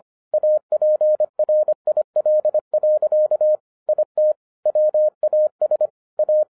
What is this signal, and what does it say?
Morse code, heard as a keyed tone, about 25 wpm, 610 Hz. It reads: APRIL. IT WAS A
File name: 2011-07-mp3 / n44 JOHN K3TN 25wpm